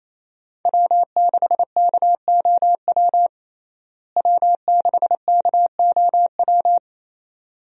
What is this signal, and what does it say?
Morse code, heard as a keyed tone, about 27 wpm, 700 Hz. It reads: W6KOW W6KOW
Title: USA Call Signs - 5 Characters - Repeated 2x - 28wpm